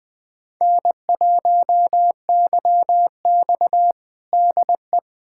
Morse code, heard as a keyed tone, about 20 wpm, 700 Hz. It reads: N1YX DE
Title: USA Callsign DE Callsign - 4 Characters - 20wpm